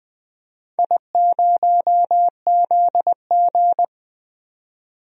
Morse code, heard as a keyed tone, about 20 wpm, 700 Hz. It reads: I0ZG
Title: DXCC-100 Call Signs - 4 Characters - 20wpm